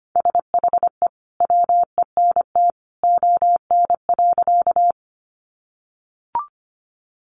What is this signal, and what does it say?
Morse code, heard as a keyed tone, about 25 wpm, 700 Hz. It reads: SHE WENT ON.
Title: Sentences from Top 200 Words 25wpm